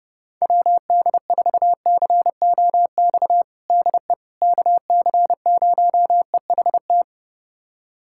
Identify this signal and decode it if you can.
Morse code, heard as a keyed tone, about 30 wpm, 700 Hz. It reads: WD4COX DE KC0EHT
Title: USA Callsign DE Callsign 30wpm